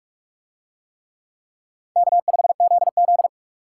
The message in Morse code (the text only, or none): K5ZZ